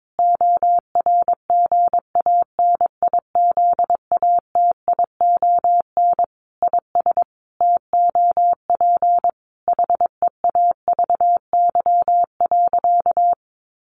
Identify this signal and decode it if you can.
Morse code, heard as a keyed tone, about 22 wpm, 700 Hz. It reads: ORGANIZATION IS TOP HEAVY.
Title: Sentences from Top 1100 Words 22wpm